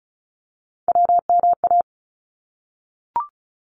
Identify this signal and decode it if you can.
Morse code, heard as a keyed tone, about 34 wpm, 700 Hz. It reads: WMA
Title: ARRL Sections 35wpm